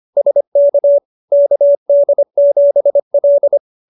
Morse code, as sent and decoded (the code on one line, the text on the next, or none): ... -.- / -.- -.. --... .-..
SK KD7L